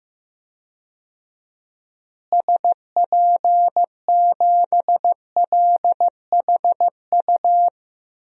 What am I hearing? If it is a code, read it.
Morse code, heard as a keyed tone, about 16 wpm, 700 Hz. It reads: SP7LHU
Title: DXCC-100 Call Signs - 6 Characters - Repeated 3x - 15wpm